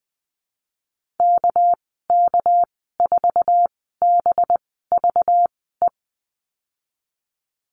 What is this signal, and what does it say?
Morse code, heard as a keyed tone, about 20 wpm, spaced as for 15 wpm, 700 Hz. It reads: KK4BVE
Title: 213 Lesson 34g - Number 8 - Call Signs 20wpm